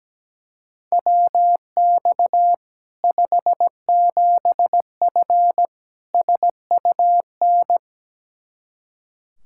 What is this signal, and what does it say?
Morse code, heard as a keyed tone, about 17 wpm, 700 Hz. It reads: WX 57F SUN
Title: K1USN SST QSO Practice 17wpm